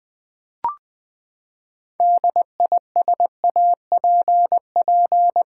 Morse code, heard as a keyed tone, about 20 wpm, 700 Hz. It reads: DISAPP
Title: Build A Word - 6 Letters At A Time - 12-Letter Words - 20wpm